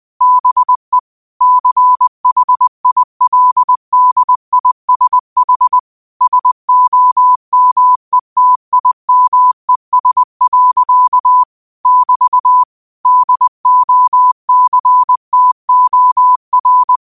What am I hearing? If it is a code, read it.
Morse code, heard as a keyed tone, about 20 wpm, 1000 Hz. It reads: BE CHILDISH SOMETIMES. <BT> DOCTOR
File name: qotdfiles / 20wpm